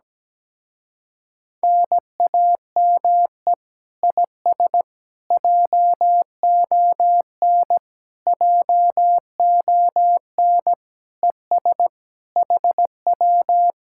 Morse code, heard as a keyed tone, about 17 wpm, 700 Hz. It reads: NAME IS JON JON ES HW
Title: Speed-Racing Common QSO Phrases Encore - 17wpm